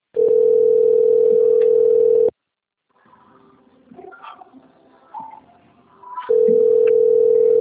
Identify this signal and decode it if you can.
Ringback tone (North America), 440 + 480 Hz.